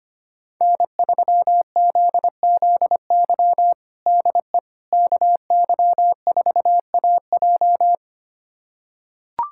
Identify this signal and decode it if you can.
Morse code, heard as a keyed tone, about 25 wpm, 700 Hz. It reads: N3ZZY DE KY4AJ
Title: USA Callsign DE Callsign - 5 Characters - 25wpm